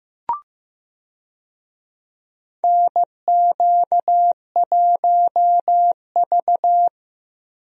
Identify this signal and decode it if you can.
Morse code, heard as a keyed tone, about 16 wpm, 700 Hz. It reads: NQ1V
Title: USA Call Signs - 4 Characters - Repeated 3x - 15wpm